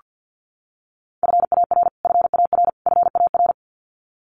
Morse code, heard as a keyed tone, about 48 wpm, 700 Hz. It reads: FAR FAR FAR
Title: Top 300 Words Repeated 50wpm